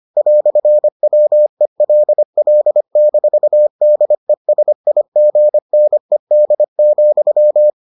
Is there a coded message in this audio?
Yes: Morse code from 0.2 to 7.7 s, heard as a keyed tone, about 25 wpm, 600 Hz. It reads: "WELL-DESIGNED,